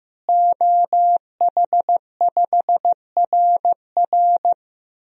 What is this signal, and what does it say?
Morse code, heard as a keyed tone, about 16 wpm, 700 Hz. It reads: OH5RR
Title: DXCC-100 Call Signs - 5 Characters - Repeated 3x - 15wpm